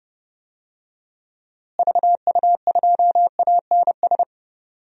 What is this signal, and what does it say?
Morse code, heard as a keyed tone, about 30 wpm, 700 Hz. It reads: VU2ANS
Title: DXCC-100 Call Signs - 6 Characters - Repeated 3x - 30wpm